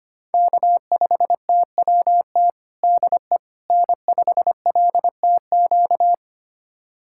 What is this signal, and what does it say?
Morse code, heard as a keyed tone, about 25 wpm, 700 Hz. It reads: K5TWT DE N5LTQ
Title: USA Callsign DE Callsign - 5 Characters - 25wpm